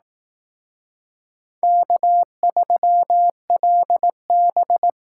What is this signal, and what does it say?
Morse code, heard as a keyed tone, about 19 wpm, 700 Hz. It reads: K3LB